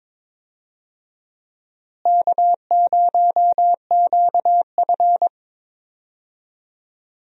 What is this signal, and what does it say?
Morse code, heard as a keyed tone, about 22 wpm, 700 Hz. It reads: K0QF
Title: USA Call Signs - 4 Characters - Repeated 2x - 22wpm